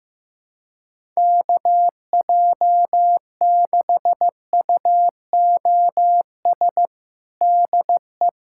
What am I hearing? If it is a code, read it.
Morse code, heard as a keyed tone, about 15 wpm, 700 Hz. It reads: KJ6UOS DE
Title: USA Callsign DE Callsign - 6 Characters - 15wpm